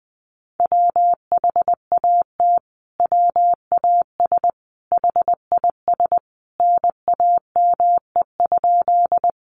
Morse code, heard as a keyed tone, about 19 wpm, 700 Hz. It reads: WHAT WAS HIS NAME?
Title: Sentences from Top 200 Words 20wpm